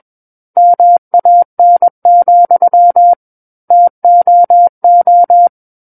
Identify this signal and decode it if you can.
Morse code, heard as a keyed tone, about 21 wpm, 700 Hz. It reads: MAN, TOO
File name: Book One / 12-Chapter 12 - 21 WPM